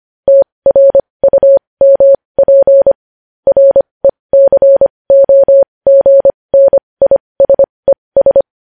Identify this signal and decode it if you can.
Morse code, heard as a keyed tone, about 25 wpm, spaced as for 20 wpm, 550 Hz. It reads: TRUMP RECOGNISES
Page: News Headlines in Morse Code at 20 WPM